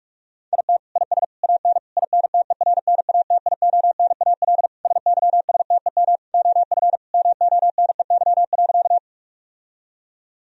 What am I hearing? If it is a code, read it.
Morse code, heard as a keyed tone, about 47 wpm, 700 Hz. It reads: IT IS AN INTERNATIONAL SYSTEM OF MONEY.
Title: Sentences from Top 1000 Words - Review Part 4 - 45wpm